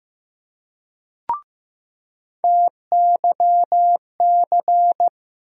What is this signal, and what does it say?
Morse code, heard as a keyed tone, about 15 wpm, 700 Hz. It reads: TYC
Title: USA Call Sign Postfixes - 3 Letters 15wpm